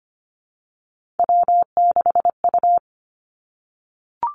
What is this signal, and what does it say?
Morse code, heard as a keyed tone, about 25 wpm, 700 Hz. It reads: W6U
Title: Three Challenging Letter-Number Combinations 25wpm